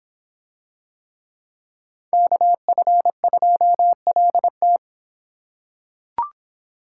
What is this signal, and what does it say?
Morse code, heard as a keyed tone, about 25 wpm, 700 Hz. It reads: KF2LT